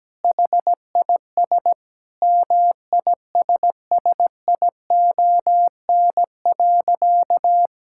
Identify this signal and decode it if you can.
Morse code, heard as a keyed tone, about 17 wpm, 700 Hz. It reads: HIS MISSION.
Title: Sentences from Top 1600 Words 17wpm